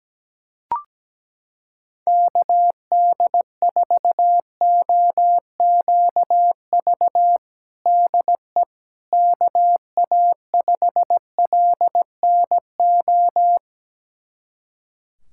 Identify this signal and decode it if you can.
Morse code, heard as a keyed tone, about 17 wpm, 700 Hz. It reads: KD4OQV DE KA5LNO
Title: USA Callsign DE Callsign - 6 Characters - 17wpm